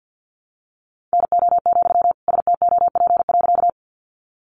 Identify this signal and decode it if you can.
Morse code, heard as a keyed tone, about 50 wpm, 700 Hz. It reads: NO, STOP.
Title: Sentences from Top 300 Words 50wpm